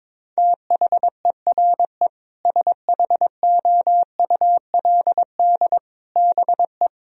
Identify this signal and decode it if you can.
Morse code, heard as a keyed tone, about 22 wpm, 700 Hz. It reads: THERE SHOULD BE
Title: Sentences from Top 1000 Words - Review Part 3 - 22wpm